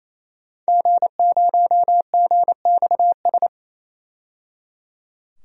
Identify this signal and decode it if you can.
Morse code, heard as a keyed tone, about 28 wpm, 700 Hz. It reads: G0GXS
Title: DXCC-100 Call Signs - 5 Characters - 28wpm